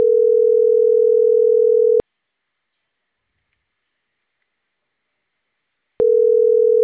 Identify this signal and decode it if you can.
Ringback tone (North America), 440 + 480 Hz.